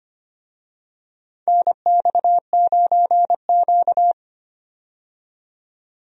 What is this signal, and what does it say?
Morse code, heard as a keyed tone, about 25 wpm, 700 Hz. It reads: NX9Q
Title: USA Call Signs - 4 Characters - 25wpm